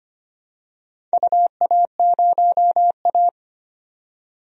Morse code, heard as a keyed tone, about 25 wpm, 700 Hz. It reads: UA0A
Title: DXCC-100 Call Signs - 4 Characters - 25wpm